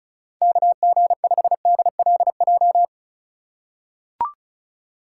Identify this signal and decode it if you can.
Morse code, heard as a keyed tone, about 36 wpm, 700 Hz. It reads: KG5DLJ
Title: USA Call Signs - 6 Characters - 2x Character Spacing - 35wpm